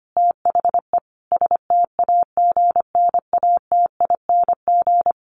Morse code, heard as a keyed tone, about 25 wpm, 700 Hz. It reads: THE STAGNATING
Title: A Sound Of Thunder 25wpm